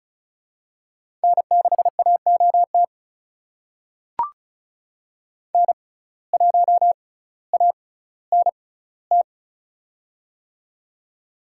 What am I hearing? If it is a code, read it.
Morse code, heard as a keyed tone, about 34 wpm, spaced as for 18 wpm, 700 Hz. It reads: N6AOT N1ANT
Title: USA Call Signs - 5 Characters - 6x Character Spacing - 35wpm